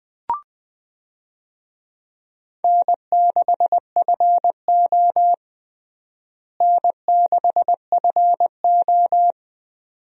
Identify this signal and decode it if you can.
Morse code, heard as a keyed tone, about 20 wpm, 700 Hz. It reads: N6FO N6FO
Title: USA Call Signs - 4 Characters - Repeated 2x - 20wpm